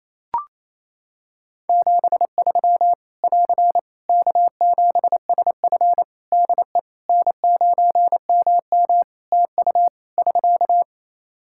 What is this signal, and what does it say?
Morse code, heard as a keyed tone, about 28 wpm, 700 Hz. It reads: 73 <AR> K7SF DE N9MM TU <SK>
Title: K1USN SST QSO Practice 28wpm